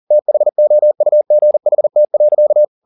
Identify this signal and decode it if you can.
Morse code, heard as a keyed tone, about 40 wpm, 600 Hz. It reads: THOUGHT.